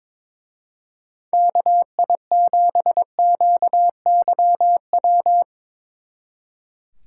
Morse code, heard as a keyed tone, about 24 wpm, 700 Hz. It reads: KI7QYW
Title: Speed Racing - USA 2x3 Call Signs 22wpm